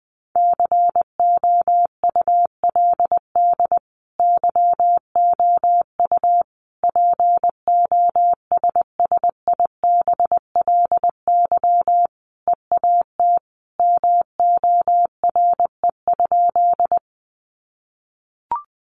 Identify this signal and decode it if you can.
Morse code, heard as a keyed tone, about 20 wpm, 700 Hz. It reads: COULD YOU POSSIBLY EAT MORE?
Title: Sentences from Top 1300 Words 20wpm